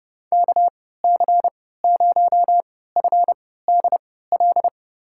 Morse code, heard as a keyed tone, about 30 wpm, spaced as for 17 wpm, 700 Hz. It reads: KC0FDL
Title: USA Call Signs - 6 Characters - 3x Character Spacing - 30wpm